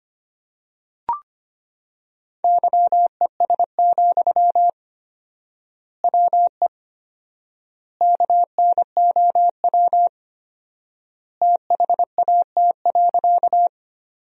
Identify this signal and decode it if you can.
Morse code, heard as a keyed tone, about 25 wpm, 700 Hz. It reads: YES, WE KNOW THAT.
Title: 4x Word-Spacing - Sentences from Top 500 Words - 25wpm